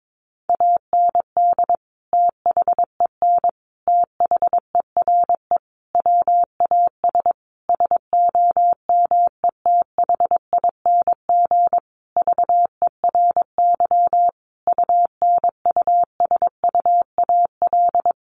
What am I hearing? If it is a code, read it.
Morse code, heard as a keyed tone, about 22 wpm, 700 Hz. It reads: AND THEN THERE WAS SOMETHING VERY UNUSUAL
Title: Sentences from Top 1400 Words 22wpm